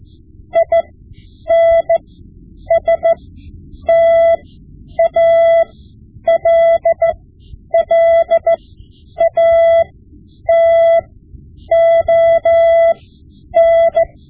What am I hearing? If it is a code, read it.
Morse code, heard as a keyed tone, about 14 wpm, spaced as for 9 wpm, 670 Hz. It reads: INSTALLATON
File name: Morse Lessons / Morse Lessons / Tape-5B